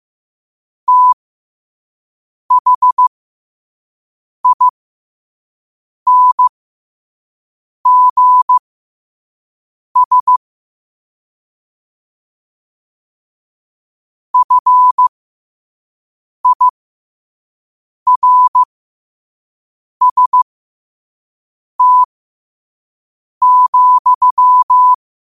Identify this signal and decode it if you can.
Morse code, heard as a keyed tone, about 15 wpm, spaced as for 5 wpm, 1000 Hz. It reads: THINGS FIRST,
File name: qotdfiles / 5wpm